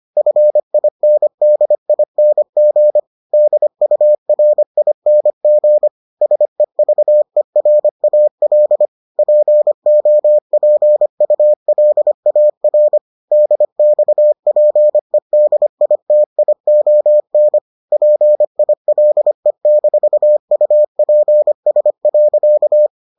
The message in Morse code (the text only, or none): FINDING DURING SEVERAL POPULAR DXPEDITION PILE-UPS.